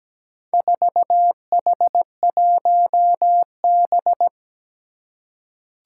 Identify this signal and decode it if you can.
Morse code, heard as a keyed tone, about 17 wpm, 700 Hz. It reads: 4H1B
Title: Four Challenging Letter-Number Combinations 17wpm